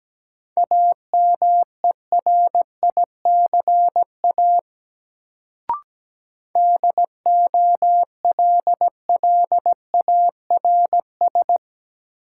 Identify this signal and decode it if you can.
Morse code, heard as a keyed tone, about 17 wpm, 700 Hz. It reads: AMERICA DOLLARS
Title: Sets of 5 Words - New Word Drill - 17wpm